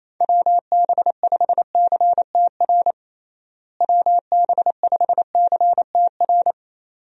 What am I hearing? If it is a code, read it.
Morse code, heard as a keyed tone, about 28 wpm, 700 Hz. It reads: WB5CTR WB5CTR
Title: USA Call Signs - 6 Characters - Repeated 2x - 28wpm